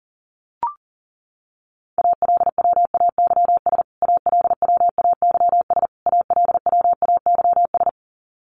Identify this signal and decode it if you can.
Morse code, heard as a keyed tone, about 40 wpm, 700 Hz. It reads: ALWAYS ALWAYS ALWAYS
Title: Top 500 Words Repeated 40wpm